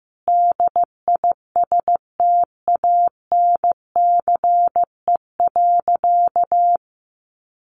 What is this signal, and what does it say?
Morse code, heard as a keyed tone, about 15 wpm, 700 Hz. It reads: DISTANCE.
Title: Sentences from Top 1200 Words 15wpm